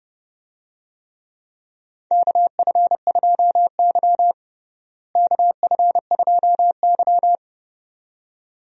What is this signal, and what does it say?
Morse code, heard as a keyed tone, about 30 wpm, 700 Hz. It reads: KF2Y KF2Y
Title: USA Call Signs - 4 Characters - Repeated 2x - 30wpm